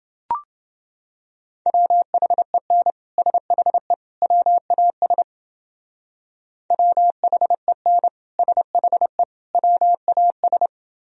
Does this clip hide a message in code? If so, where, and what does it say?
Yes: Morse code from 1.7 to 10.7 s, heard as a keyed tone, about 29 wpm, 700 Hz. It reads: WHEN SHE WAS WHEN SHE WAS